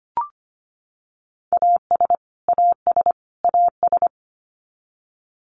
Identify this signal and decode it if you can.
Morse code, heard as a keyed tone, about 25 wpm, 700 Hz. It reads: AS AS AS
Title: Top 200 Words Repeated 25wpm